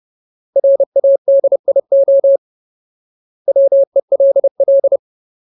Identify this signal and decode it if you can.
Morse code, heard as a keyed tone, about 30 wpm, 550 Hz. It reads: RADIO WELL
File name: Ultimate Word List / 30WPM Ultimate